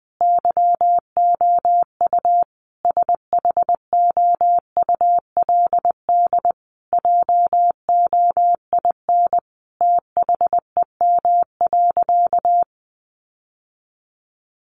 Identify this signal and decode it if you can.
Morse code, heard as a keyed tone, about 20 wpm, 700 Hz. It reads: YOU SHOULD JOIN THEM.
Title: Sentences from Top 1400 Words 20wpm